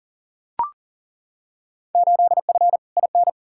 Morse code, heard as a keyed tone, about 40 wpm, 700 Hz. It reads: 8F IN
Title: Field Day Exchanges Encore - 40wpm